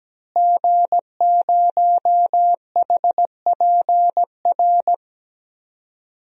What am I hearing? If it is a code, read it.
Morse code, heard as a keyed tone, about 17 wpm, 700 Hz. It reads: G0HPR
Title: DXCC-100 Call Signs - 5 Characters - Repeated 3x - 17wpm